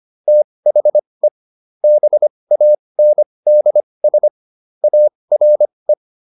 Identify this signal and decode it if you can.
Morse code, heard as a keyed tone, about 25 wpm, spaced as for 20 wpm, 600 Hz. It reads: THE BANDS ARE